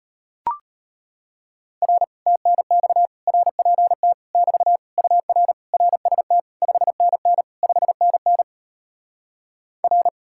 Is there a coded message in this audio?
Yes: Morse code from 1.8 to 10.1 s, heard as a keyed tone, about 33 wpm, 700 Hz. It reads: R TNX RPT <BT> UR RST 5NN 5NN R